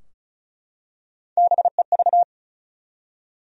Morse code, heard as a keyed tone, about 36 wpm, 700 Hz. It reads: BEV
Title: USA Common Names - Max 4 Letters - 35wpm